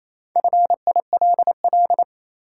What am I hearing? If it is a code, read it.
Morse code, heard as a keyed tone, about 28 wpm, 700 Hz. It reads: FILL
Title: Sentences from Top 1500 Words - New Word Drill Alt - 28wpm